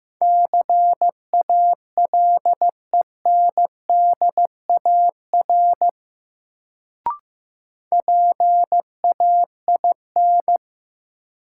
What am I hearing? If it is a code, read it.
Morse code, heard as a keyed tone, about 15 wpm, 700 Hz. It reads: CALENDAR PAIN
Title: Sentences from Top 1600 Words - New Word Drill Alt - 15wpm